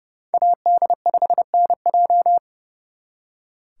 Morse code, heard as a keyed tone, about 37 wpm, 700 Hz. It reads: AD5NJ
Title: Speed Racing - USA 2x2 Call Signs 30wpm